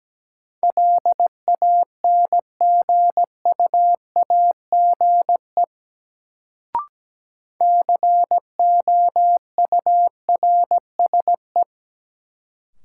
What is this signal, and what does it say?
Morse code, heard as a keyed tone, about 17 wpm, 700 Hz. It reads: LANGUAGE COURSE
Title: Sentences from Top 500 Words - New Word Drill - 17wpm